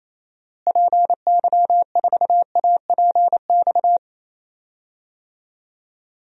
Morse code, heard as a keyed tone, about 28 wpm, 700 Hz. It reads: PY4APX
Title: DXCC-100 Call Signs - 6 Characters - Repeated 3x - 28wpm